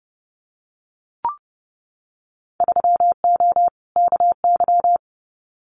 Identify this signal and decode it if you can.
Morse code, heard as a keyed tone, about 30 wpm, 700 Hz. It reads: 3O KY
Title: Winter Field Day Exchanges 30wpm